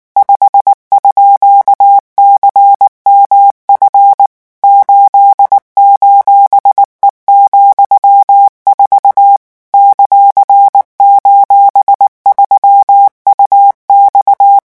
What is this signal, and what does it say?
Morse code, heard as a keyed tone, about 19 wpm, 800 Hz. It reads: 5_CMF 8:E,4 ;:3UX
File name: morse mp3 / revision14-vitesse 18 mots